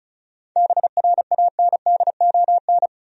Morse code, heard as a keyed tone, about 36 wpm, 700 Hz. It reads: BRANDON
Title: USA Common Names - 35wpm